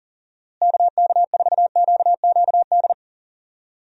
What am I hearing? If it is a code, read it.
Morse code, heard as a keyed tone, about 38 wpm, 700 Hz. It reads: KK4QQD
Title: USA Call Signs - 6 Characters - Repeated 2x - 40wpm